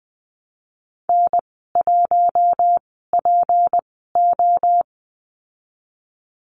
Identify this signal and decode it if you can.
Morse code, heard as a keyed tone, about 19 wpm, spaced as for 14 wpm, 700 Hz. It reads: N1PO
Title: 154 Lesson 25g - Letter p - Call Signs 20wpm